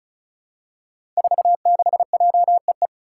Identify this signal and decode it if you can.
Morse code, heard as a keyed tone, about 36 wpm, 700 Hz. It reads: 46JEE
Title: Five Challenging Letter-Number Combinations 35wpm